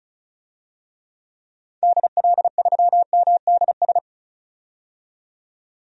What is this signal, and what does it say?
Morse code, heard as a keyed tone, about 36 wpm, 700 Hz. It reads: DL3MDS
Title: DXCC-100 Call Signs - 6 Characters - 35wpm